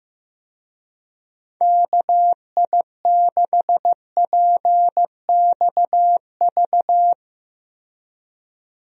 Morse code, heard as a keyed tone, about 15 wpm, 700 Hz. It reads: KI6PXV
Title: USA Call Signs - 6 Characters - 15wpm